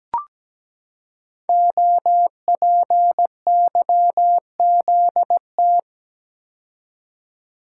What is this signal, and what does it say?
Morse code, heard as a keyed tone, about 17 wpm, 700 Hz. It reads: OPYZT
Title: Code Groups - 5 Letters - 17wpm